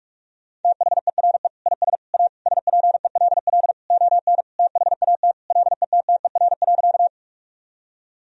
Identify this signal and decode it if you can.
Morse code, heard as a keyed tone, about 45 wpm, 700 Hz. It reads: THERE IS A SPELL ON THAT LETTER.
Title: Sentences from Top 1000 Words - Review Part 1 - 45wpm